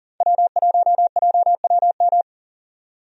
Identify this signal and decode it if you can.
Morse code, heard as a keyed tone, about 40 wpm, 700 Hz. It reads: W1JWM
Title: USA Call Signs - 5 Characters - 40wpm